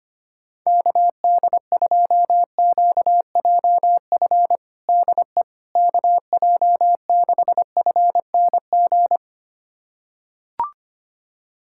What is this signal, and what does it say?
Morse code, heard as a keyed tone, about 25 wpm, 700 Hz. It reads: KD2QJF DE KJ6FNG
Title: USA Callsign DE Callsign - 6 Characters - 25wpm